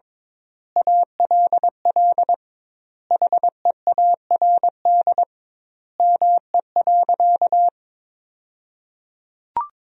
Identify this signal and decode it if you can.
Morse code, heard as a keyed tone, about 22 wpm, 700 Hz. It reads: ALL HEARD ME.
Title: 2x Word-Spacing - Sentences from Top 400 Words - 22wpm